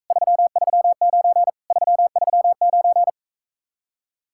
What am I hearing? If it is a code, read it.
Morse code, heard as a keyed tone, about 45 wpm, 700 Hz. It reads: 339 339